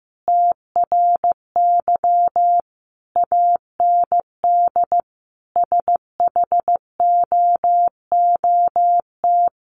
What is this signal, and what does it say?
Morse code, heard as a keyed tone, about 14 wpm, 700 Hz. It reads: TRY AND SHOOT
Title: Sentences from Top 1200 Words 15wpm